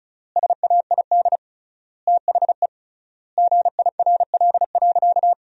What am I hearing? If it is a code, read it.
Morse code, heard as a keyed tone, about 36 wpm, 700 Hz. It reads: SAID THE GIRL.
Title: 3x Word-Spacing - Sentences from Top 400 Words - 35wpm